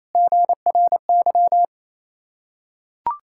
Morse code, heard as a keyed tone, about 28 wpm, 700 Hz. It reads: GRY
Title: Ontario QSO Party Exchange 28wpm